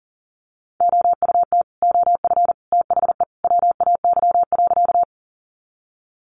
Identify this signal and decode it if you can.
Morse code, heard as a keyed tone, about 40 wpm, 700 Hz. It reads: OUT OF THE WAY.
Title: Sentences from Top 100 Words 40wpm